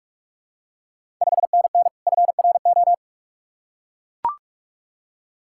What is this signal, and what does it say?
Morse code, heard as a keyed tone, about 44 wpm, 690 Hz. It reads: 5NN FRO
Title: Ontario QSO Party Exchange 45wpm